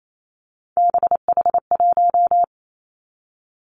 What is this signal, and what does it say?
Morse code, heard as a keyed tone, about 28 wpm, 690 Hz. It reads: BH1
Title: Three Challenging Letter-Number Combinations 28wpm